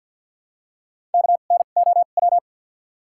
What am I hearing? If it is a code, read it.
Morse code, heard as a keyed tone, about 53 wpm, spaced as for 37 wpm, 700 Hz. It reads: KNOW
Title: 2x Character Spacing - Top 100 Words - 50wpm